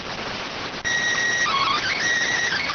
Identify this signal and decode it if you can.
SSTV (slow-scan television), PD 290.